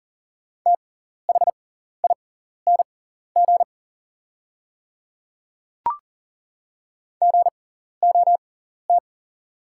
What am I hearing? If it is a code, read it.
Morse code, heard as a keyed tone, about 38 wpm, spaced as for 14 wpm, 700 Hz. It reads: THING GOT
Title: 6x Character Spacing - Top 300 Words - 40wpm